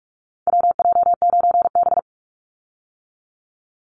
Morse code, heard as a keyed tone, about 44 wpm, 700 Hz. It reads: WJ9B
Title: Call Signs - 45wpm